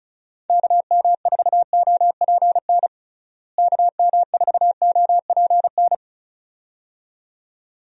Morse code, heard as a keyed tone, about 34 wpm, 700 Hz. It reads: KM4OPN KM4OPN
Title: USA Call Signs - 6 Characters - Repeated 2x - 35wpm